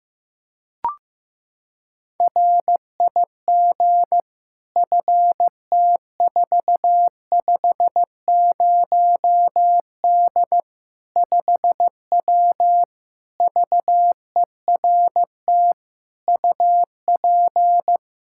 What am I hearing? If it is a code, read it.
Morse code, heard as a keyed tone, about 15 wpm, 700 Hz. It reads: RIG FT450D 5W VERT UP
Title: Practice QSO - Exchange 2 - Info - 15wpm